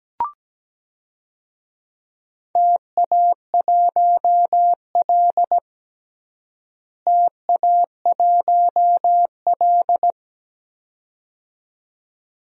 Morse code, heard as a keyed tone, about 17 wpm, 700 Hz. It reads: TA1L TA1L
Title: DXCC-100 Call Signs - 4 Characters - Repeated 2x - 17wpm